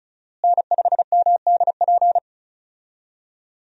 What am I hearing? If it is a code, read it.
Morse code, heard as a keyed tone, about 36 wpm, 700 Hz. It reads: N5MDP
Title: USA Call Signs - 5 Characters - Repeated 3x - 35wpm